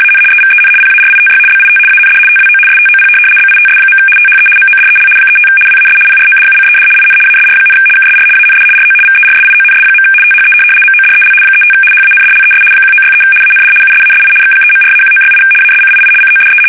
RTTY, about 75 baud. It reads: AN EVEN PARITY BIT AND TWO STOP BITS. THE MODEL 35 WAS PRODUCED IN ASR, KSR AND RO VERSIONS. THE MODEL 35 HANDLES 1963 AND USASI X3.4-1968 ASCII CODE
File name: fsk850-75B / teletype-corp